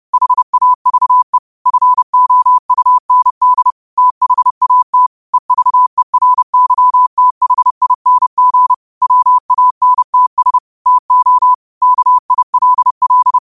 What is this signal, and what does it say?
Morse code, heard as a keyed tone, about 30 wpm, 1000 Hz. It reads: HAVE FOUND THAT EVERYTHING WANTS TO KILL
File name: qotdfiles / 30wpm